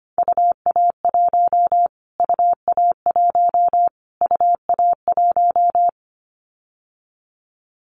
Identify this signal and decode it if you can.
Morse code, heard as a keyed tone, about 25 wpm, 700 Hz. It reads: UA1 UA1 UA1
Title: Callsign Prefixes for Top 25 DXCC - Repeated 3x - 25wpm